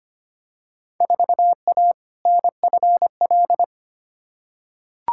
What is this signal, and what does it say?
Morse code, heard as a keyed tone, about 25 wpm, 700 Hz. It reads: 4A NFL
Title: Field Day Exchanges - 25wpm